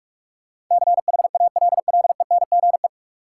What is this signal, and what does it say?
Morse code, heard as a keyed tone, about 46 wpm, 700 Hz. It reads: CHALLENGE
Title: Sentences from Top 1300 Words - New Word Drill Alt - 45wpm